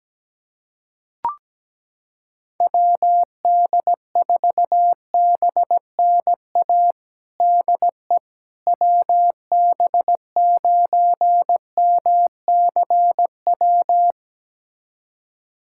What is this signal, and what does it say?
Morse code, heard as a keyed tone, about 17 wpm, 700 Hz. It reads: WD4BNA DE WB9MCW
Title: USA Callsign DE Callsign - 6 Characters - 17wpm